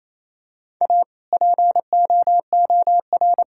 Morse code, heard as a keyed tone, about 27 wpm, 700 Hz. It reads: A POOR
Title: Sentences from Top 1600 Words 28wpm